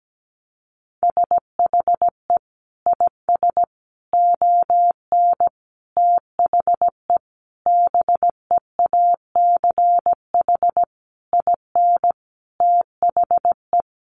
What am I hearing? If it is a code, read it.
Morse code, heard as a keyed tone, about 17 wpm, 700 Hz. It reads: SHE IS ON THE BEACH IN THE
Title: Sentences from Top 1500 Words 17wpm